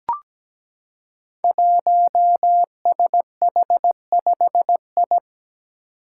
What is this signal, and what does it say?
Morse code, heard as a keyed tone, about 17 wpm, 700 Hz. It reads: 1SH5I
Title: Five Challenging Letter-Number Combinations 17wpm